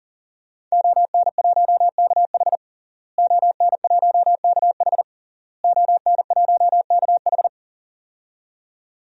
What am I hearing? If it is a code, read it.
Morse code, heard as a keyed tone, about 37 wpm, 700 Hz. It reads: ON1KH ON1KH ON1KH
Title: DXCC-100 Call Signs - 5 Characters - Repeated 3x - 40wpm